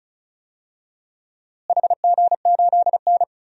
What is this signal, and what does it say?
Morse code, heard as a keyed tone, about 36 wpm, 700 Hz. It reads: HG8N
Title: DXCC-100 Call Signs - 4 Characters - Repeated 2x - 35wpm